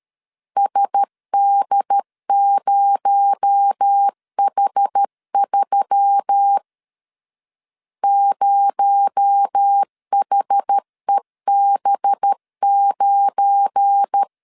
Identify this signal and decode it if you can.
Morse code, heard as a keyed tone, about 12 wpm, 800 Hz. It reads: SD0H3 0HEB9
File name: LEZIONI / 7 GF9Z / 7 GF9 12wpm 2319